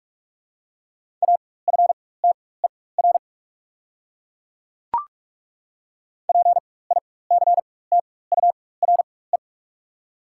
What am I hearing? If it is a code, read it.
Morse code, heard as a keyed tone, about 42 wpm, spaced as for 21 wpm, 690 Hz. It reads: AFTER PICTURE
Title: 4x Character Spacing - Top 300 Words - 45wpm